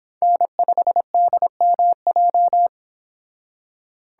Morse code, heard as a keyed tone, about 26 wpm, 700 Hz. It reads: N5DMJ